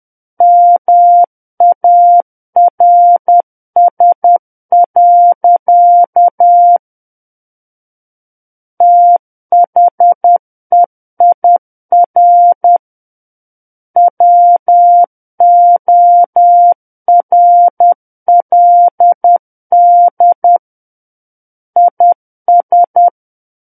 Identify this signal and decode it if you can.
Morse code, heard as a keyed tone, about 10 wpm, 700 Hz. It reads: MARS. THEIR WORLD IS
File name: Book One / 01-Chapter 1 - 10 WPM